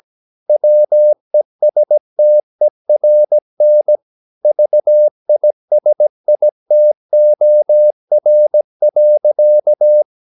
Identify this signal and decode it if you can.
Morse code, heard as a keyed tone, about 17 wpm, 600 Hz. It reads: WESTERN VISITOR.
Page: Romanen Never af Ken Follet på engelsk.